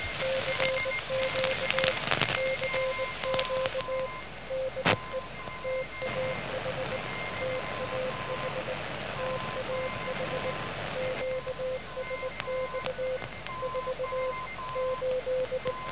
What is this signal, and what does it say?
Morse code, heard as a keyed tone, about 19 wpm, 510 Hz, with noise. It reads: CQ CQ DE TAH TAH TAH QSX 4 8
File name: TAH 17008.5 0900